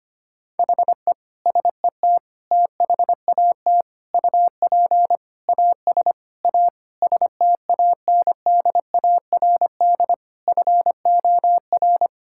Morse code, heard as a keyed tone, about 25 wpm, 700 Hz. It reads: HE SET THAT UP AS A STANDARD FOR
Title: Sentences from Top 1000 Words - Review Part 6 - 25wpm